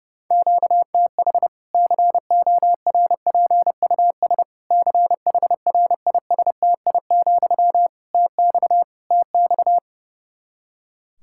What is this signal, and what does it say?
Morse code, heard as a keyed tone, about 30 wpm, 700 Hz. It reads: QTH CORPUS CHRISTI, TX TX
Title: Practice QSO - Exchange 1 - Beginning - 30wpm